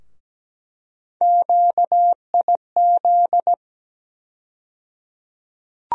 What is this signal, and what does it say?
Morse code, heard as a keyed tone, about 17 wpm, 700 Hz. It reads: QIZ
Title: USA Call Sign Postfixes - 3 Letters 17wpm